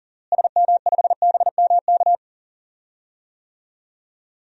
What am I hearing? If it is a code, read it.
Morse code, heard as a keyed tone, about 40 wpm, 700 Hz. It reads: SM5BMK
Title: DXCC-100 Call Signs - 6 Characters - Repeated 2x - 40wpm